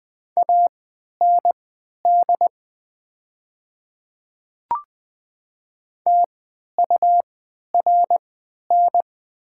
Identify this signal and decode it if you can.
Morse code, heard as a keyed tone, about 20 wpm, spaced as for 11 wpm, 700 Hz. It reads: AND TURN
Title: 3x Character Spacing - Top 300 Words - 20wpm